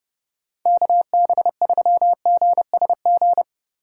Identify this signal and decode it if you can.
Morse code, heard as a keyed tone, about 40 wpm, 700 Hz. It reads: KB3GSG